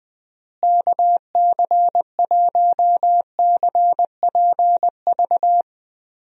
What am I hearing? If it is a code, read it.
Morse code, heard as a keyed tone, about 20 wpm, 700 Hz. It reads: KC1CPV
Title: USA Call Signs - 6 Characters - Repeated 3x - 20wpm